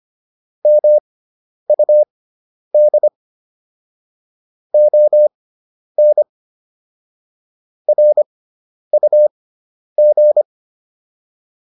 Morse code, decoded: MUD ON RUG

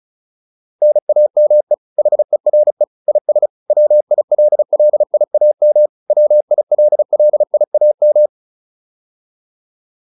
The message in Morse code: NAME HERE IS WILLIAM WILLIAM

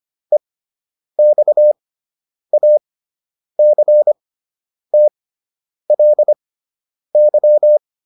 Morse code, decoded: EXACTLY

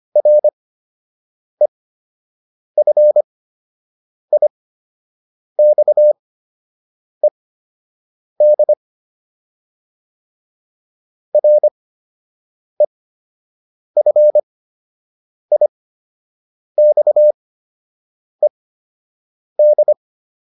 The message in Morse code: REFIXED REFIXED

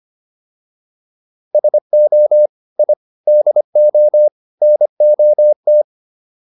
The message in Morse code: SO I DO NOT

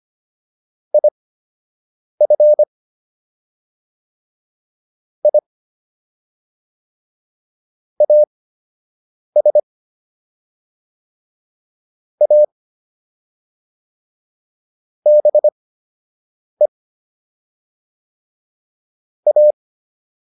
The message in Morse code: IF I AS A BE A

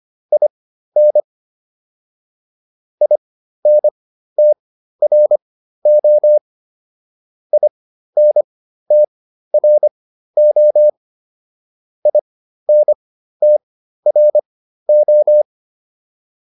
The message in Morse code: IN INTRO INTRO INTRO

